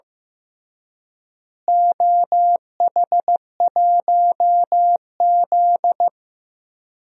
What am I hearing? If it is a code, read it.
Morse code, heard as a keyed tone, about 15 wpm, 700 Hz. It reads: OH1Z
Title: DXCC-100 Call Signs - 4 Characters - Repeated 2x - 15wpm